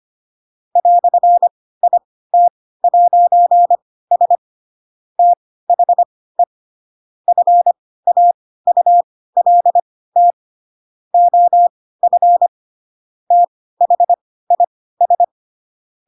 Morse code, heard as a keyed tone, about 25 wpm, spaced as for 16 wpm, 700 Hz. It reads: "IT'S THE FAULT OF THIS
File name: Pinocchio / Pinocchio ch2 10 25 ik1hge